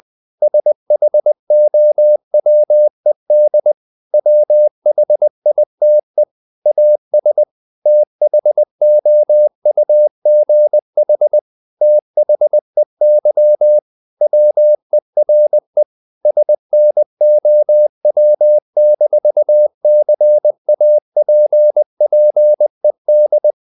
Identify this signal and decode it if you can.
Morse code, heard as a keyed tone, about 20 wpm, 600 Hz. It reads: SHOWED WHITE AS THOUGH THEY WERE SNOW-CAPPED
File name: the old man and the sea / the old man and the sea part 1 20wpm